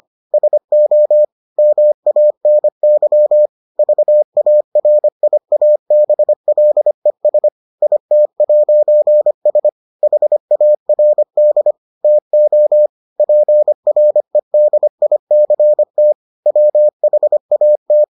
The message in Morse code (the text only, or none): SO MANY VARIABLES IT'S HARD TO PREDICT WHAT